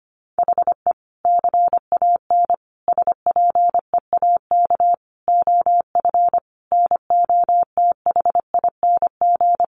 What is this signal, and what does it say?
Morse code, heard as a keyed tone, about 25 wpm, 700 Hz. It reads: HE CAN SPEAK OF NOTHING
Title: Sentences from Top 1100 Words 25wpm